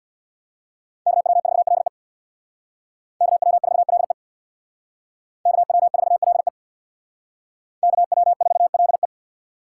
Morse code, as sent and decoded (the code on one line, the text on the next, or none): -.- .-- ....- .-.. / -.- .-- ....- .-.. / -.- .-- ....- .-.. / -.- .-- ....- .-.. .
KW4LE KW4LE KW4LE KW4LE